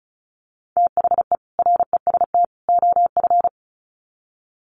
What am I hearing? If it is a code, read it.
Morse code, heard as a keyed tone, about 34 wpm, 700 Hz. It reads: THE REST OF
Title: Sets of 3 Words 35wpm